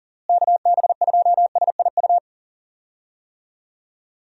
Morse code, heard as a keyed tone, about 40 wpm, 700 Hz. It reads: KB2SIU
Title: USA Call Signs - 6 Characters - Repeated 2x - 40wpm